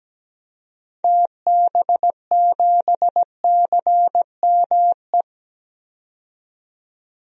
Morse code, heard as a keyed tone, about 17 wpm, 700 Hz. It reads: TB7CME
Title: DXCC-100 Call Signs - 6 Characters - 17wpm